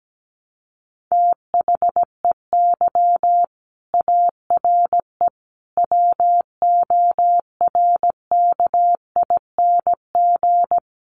Morse code, heard as a keyed tone, about 17 wpm, 700 Hz. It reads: THEY ARE WORKING
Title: Sentences from Top 700 Words 17wpm